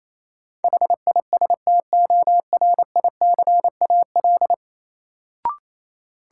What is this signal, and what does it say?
Morse code, heard as a keyed tone, about 28 wpm, 700 Hz. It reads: HISTORICAL
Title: Sentences from Top 600 Words - New Word Drill - 28wpm